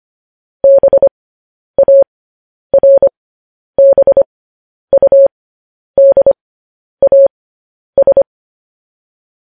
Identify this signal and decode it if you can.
Morse code, heard as a keyed tone, about 25 wpm, spaced as for 10 wpm, 550 Hz. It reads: BARBUDAS
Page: news-headlines-in-morse-code-at-10-wpm